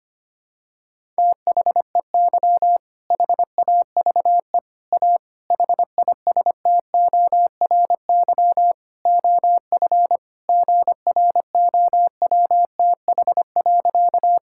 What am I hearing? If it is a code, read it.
Morse code, heard as a keyed tone, about 25 wpm, 700 Hz. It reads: THEY HAVE A HISTORY OF GROWTH.
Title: Sentences from Top 1000 Words - Review Part 5 - 25wpm